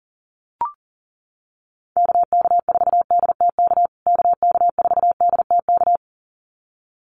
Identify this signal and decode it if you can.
Morse code, heard as a keyed tone, about 40 wpm, 700 Hz. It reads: KK4DTK KK4DTK
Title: USA Call Signs Repeated 2x 40wpm